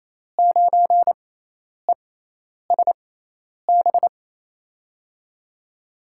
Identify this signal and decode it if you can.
Morse code, heard as a keyed tone, about 28 wpm, spaced as for 10 wpm, 700 Hz. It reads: 9ESB
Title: USA Call Signs - 5 Characters - 6x Character Spacing - 28wpm